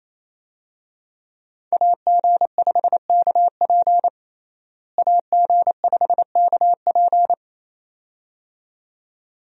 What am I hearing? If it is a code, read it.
Morse code, heard as a keyed tone, about 28 wpm, 700 Hz. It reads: AG5KP AG5KP
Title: USA Call Signs - 5 Characters - Repeated 2x - 28wpm